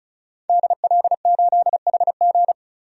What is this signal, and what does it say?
Morse code, heard as a keyed tone, about 36 wpm, 700 Hz. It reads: DL8HG
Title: DXCC-100 Call Signs - 5 Characters - 35wpm